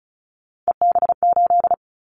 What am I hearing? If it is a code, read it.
Morse code, heard as a keyed tone, about 36 wpm, 690 Hz. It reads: EB8
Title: Callsign Prefixes for Top 50 DXCC - 35wpm